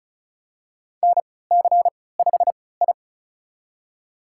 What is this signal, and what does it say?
Morse code, heard as a keyed tone, about 36 wpm, spaced as for 20 wpm, 700 Hz. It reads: NC5I
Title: USA Call Signs - 4 Characters - 3x Character Spacing - 35wpm